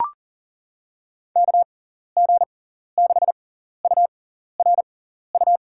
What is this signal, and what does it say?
Morse code, heard as a keyed tone, about 37 wpm, spaced as for 14 wpm, 700 Hz. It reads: KG6URU
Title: USA Call Signs - 6x Character Spacing - 40wpm